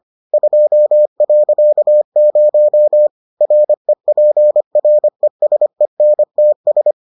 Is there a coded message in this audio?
Yes: Morse code from 0.3 to 6.9 s, heard as a keyed tone, about 25 wpm, 600 Hz. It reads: 2.0 REPRESENTS